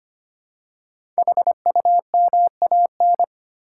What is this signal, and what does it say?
Morse code, heard as a keyed tone, about 25 wpm, 700 Hz. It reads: HUMAN
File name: 5 Letter Words - First 3 Letters Given - 25wpm